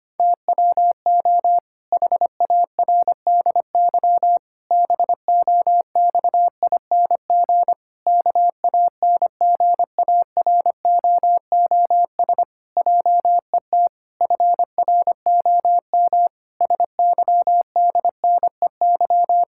Morse code, heard as a keyed tone, about 25 wpm, 700 Hz. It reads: TWO HARDY BOXING KANGAROOS JET FROM SYDNEY
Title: Pangram Challenge v2 - 25wpm